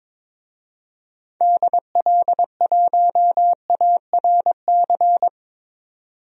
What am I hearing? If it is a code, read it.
Morse code, heard as a keyed tone, about 22 wpm, 700 Hz. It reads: DL1ARC
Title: DXCC-100 Call Signs - 6 Characters - 22wpm